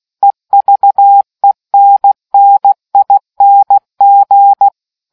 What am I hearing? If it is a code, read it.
Morse code, heard as a keyed tone, about 16 wpm, 800 Hz. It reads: EVENNING